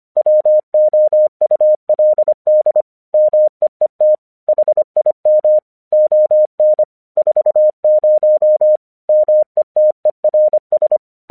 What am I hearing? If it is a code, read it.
Morse code, heard as a keyed tone, about 25 wpm, 610 Hz. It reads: WOULD MEET HIM ON 40 METERS
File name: 1 English 25wpm / AQC36